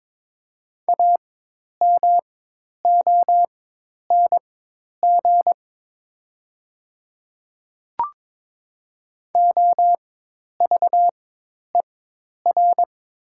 Morse code, heard as a keyed tone, about 22 wpm, spaced as for 10 wpm, 700 Hz. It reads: AMONG OVER
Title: 4x Character Spacing - Top 500 Words - 22wpm